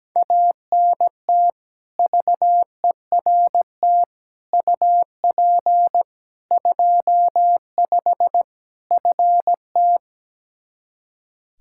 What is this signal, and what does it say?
Morse code, heard as a keyed tone, about 17 wpm, 700 Hz. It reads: ANT VERT UP 25 FT
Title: K1USN SST QSO Practice 17wpm